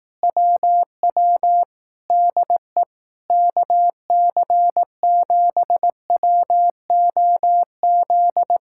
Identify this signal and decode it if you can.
Morse code, heard as a keyed tone, about 18 wpm, 700 Hz. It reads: WW DE KC7WOZ
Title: Speed-Racing Common QSO Phrases Encore - 15wpm